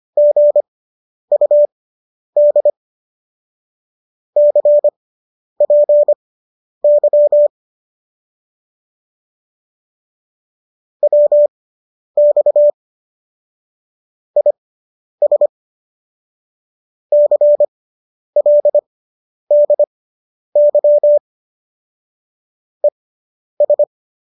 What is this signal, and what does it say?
Morse code, heard as a keyed tone, about 25 wpm, spaced as for 10 wpm, 590 Hz. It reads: GUD CPY WX IS CLDY ES